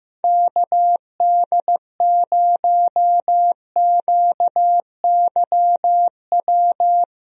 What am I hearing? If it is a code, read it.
Morse code, heard as a keyed tone, about 15 wpm, 700 Hz. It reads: KD0QYW
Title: USA Call Signs - 6 Characters - Repeated 2x - 15wpm